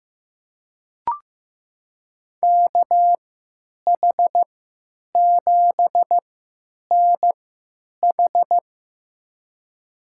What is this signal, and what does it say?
Morse code, heard as a keyed tone, about 15 wpm, spaced as for 9 wpm, 700 Hz. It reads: KH7NH
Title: USA Call Signs - 3x Character Spacing - 15wpm